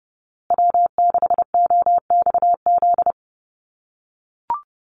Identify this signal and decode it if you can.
Morse code, heard as a keyed tone, about 30 wpm, 700 Hz. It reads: W6OXZ
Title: USA Call Signs Repeated 2x 30wpm